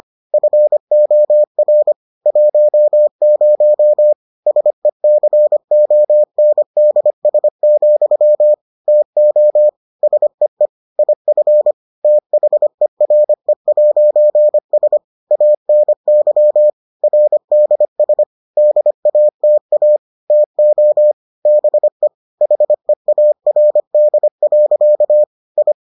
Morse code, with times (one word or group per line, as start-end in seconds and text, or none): 0.3-1.9 FOR
2.3-4.1 10
4.5-8.5 SECONDS,
8.9-9.7 TO
10.0-10.7 SEE
11.0-11.7 IF
12.0-15.0 THERE'S
15.3-16.7 ANY
17.0-18.2 RDS
18.6-20.0 DATA
20.3-21.1 TO
21.5-22.1 BE
22.4-25.2 HEARD.
25.6-25.7 I